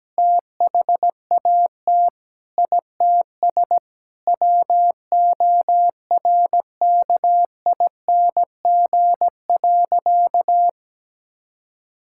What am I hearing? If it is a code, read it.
Morse code, heard as a keyed tone, about 17 wpm, 700 Hz. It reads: THAT ITS WORKING.
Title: Sentences from Top 1000 Words - Review Part 6 - 17wpm